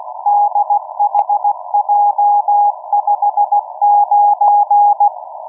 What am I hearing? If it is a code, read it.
Morse code, heard as a keyed tone, about 16 wpm, 820 Hz. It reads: DHJ59
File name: europe / d-3086-bundesmarine wilhelmshaven'g23b-dhj59'-cw-(1986)-(rb)-(mc-913-2-25)